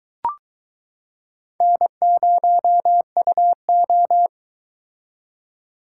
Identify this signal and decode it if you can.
Morse code, heard as a keyed tone, about 23 wpm, 700 Hz. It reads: N0UO